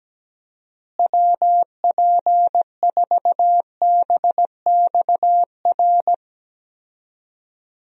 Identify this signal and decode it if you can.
Morse code, heard as a keyed tone, about 17 wpm, 700 Hz. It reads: WP4BXR
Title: USA Call Signs - 6 Characters - Repeated 2x - 17wpm